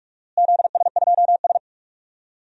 Morse code, heard as a keyed tone, about 45 wpm, 700 Hz. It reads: ZS2S
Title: DXCC-100 Call Signs - 4 Characters - Repeated 2x - 45wpm